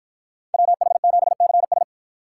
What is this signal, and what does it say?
Morse code, heard as a keyed tone, about 53 wpm, 700 Hz. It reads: WH7CS